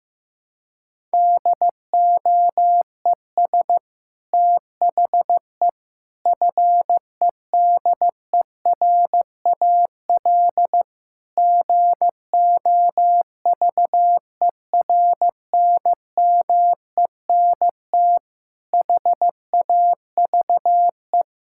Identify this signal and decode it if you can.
Morse code, heard as a keyed tone, about 16 wpm, 700 Hz. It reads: DOES THE FEDERAL GOVERNMENT HAVE
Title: Sentences from Top 1000 Words - Review Part 5 - 15wpm